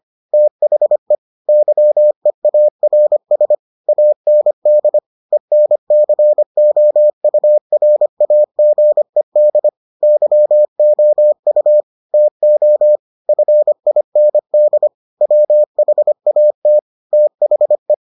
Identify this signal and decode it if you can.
Morse code, heard as a keyed tone, about 25 wpm, 600 Hz. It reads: THE YEARS AND ENCOURAGED YOU TO FIND WHAT THE